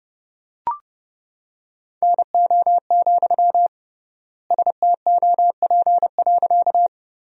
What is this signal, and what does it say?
Morse code, heard as a keyed tone, about 32 wpm, 700 Hz. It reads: NO, STOP.
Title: 3x Word-Spacing - Sentences from Top 300 Words - 30wpm